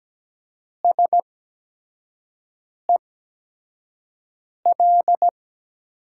Morse code, heard as a keyed tone, about 17 wpm, spaced as for 5 wpm, 700 Hz. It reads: SEL
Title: USA Call Signs - 5 Characters - 8x Character Spacing - 17wpm